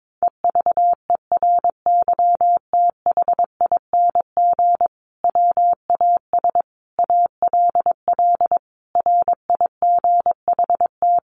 Morse code, heard as a keyed tone, about 22 wpm, 700 Hz. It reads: EVERYTHING WAS ALL RIGHT
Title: Sentences from Top 700 Words 22wpm